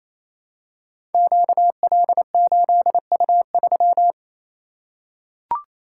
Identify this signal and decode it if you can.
Morse code, heard as a keyed tone, about 28 wpm, 700 Hz. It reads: QL8U3
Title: Code Groups - 5 Letter:Numbers - 28wpm